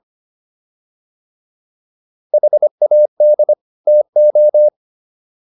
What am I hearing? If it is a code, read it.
Morse code, heard as a keyed tone, about 25 wpm, 600 Hz. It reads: HAD TO
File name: WD104 25